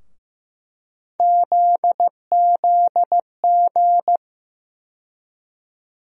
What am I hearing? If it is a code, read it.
Morse code, heard as a keyed tone, about 16 wpm, 700 Hz. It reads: ZZG
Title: USA Call Sign Postfixes - 3 Letters 15wpm